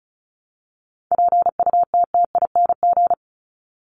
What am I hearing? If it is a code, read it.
Morse code, heard as a keyed tone, about 34 wpm, 700 Hz. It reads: PUTTING
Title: ING Suffix - Top 100 Words with Base Words 35wpm